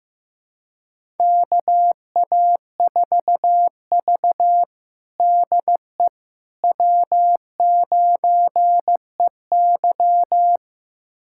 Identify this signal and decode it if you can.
Morse code, heard as a keyed tone, about 16 wpm, 700 Hz. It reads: KA4V DE W9EY
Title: USA Callsign DE Callsign - 4 Characters - 15wpm